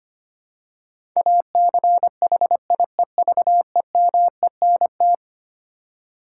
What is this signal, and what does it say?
Morse code, heard as a keyed tone, about 25 wpm, 700 Hz. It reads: ACHIEVEMENT
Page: morse-code-ninja-practice-25wpm